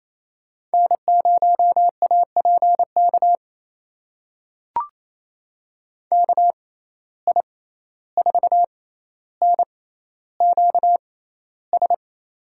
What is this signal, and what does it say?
Morse code, heard as a keyed tone, about 28 wpm, spaced as for 14 wpm, 700 Hz. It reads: N0APK KI4NQS
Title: USA Call Signs - 6x Character Spacing - 28wpm